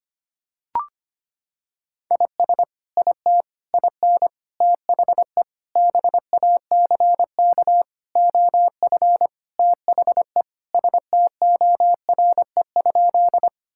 Morse code, heard as a keyed tone, about 25 wpm, 700 Hz. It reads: IS IT IN THE BACK OF THE STORE?
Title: Sentences from Top 1000 Words - Review Part 5 - 25wpm